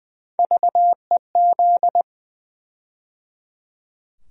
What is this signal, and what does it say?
Morse code, heard as a keyed tone, about 20 wpm, 700 Hz. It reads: VEZ
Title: USA Call Sign Postfixes - 3 Letters 20wpm